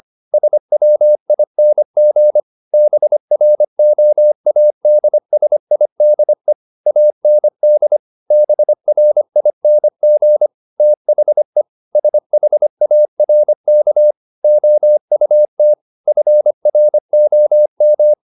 Morse code, heard as a keyed tone, about 25 wpm, 600 Hz. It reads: SWING BROADSIDE AND BRING THE SHARK OUT FROM